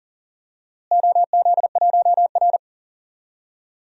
Morse code, heard as a keyed tone, about 40 wpm, 700 Hz. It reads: OZ1R
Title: DXCC-100 Call Signs - 4 Characters - Repeated 3x - 40wpm